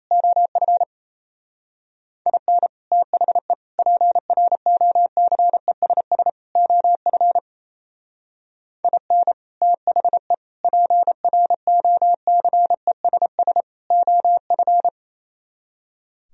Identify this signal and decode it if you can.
Morse code, heard as a keyed tone, about 35 wpm, 700 Hz. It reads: OF IN THE PROCESS OF IN THE PROCESS OF